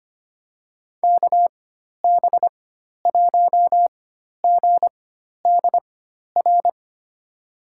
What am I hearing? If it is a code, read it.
Morse code, heard as a keyed tone, about 25 wpm, spaced as for 12 wpm, 700 Hz. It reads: KB1GDR
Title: USA Call Signs - 6 Characters - 4x Character Spacing - 25wpm